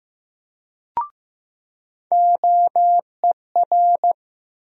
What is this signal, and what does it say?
Morse code, heard as a keyed tone, about 15 wpm, 700 Hz. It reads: OER
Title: USA Call Sign Postfixes - 3 Letters 15wpm